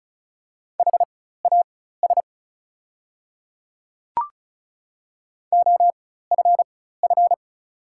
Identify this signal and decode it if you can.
Morse code, heard as a keyed tone, about 36 wpm, spaced as for 16 wpm, 690 Hz. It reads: HAS OFF
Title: 4x Character Spacing - Top 200 Words - 35wpm